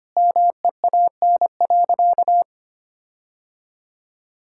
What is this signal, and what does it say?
Morse code, heard as a keyed tone, about 25 wpm, 700 Hz. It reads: MEAN.
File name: practice / 25wpm / Sentences with 4-Word Phrases - 25wpm